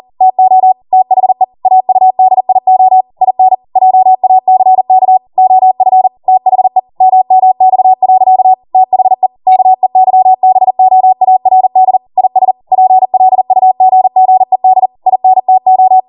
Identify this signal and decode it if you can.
Morse code, heard as a keyed tone, about 40 wpm, 750 Hz. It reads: TO THE AUDIO IN JACK OF THE MMX. THE KEYBOARD IS PLUGGED INTO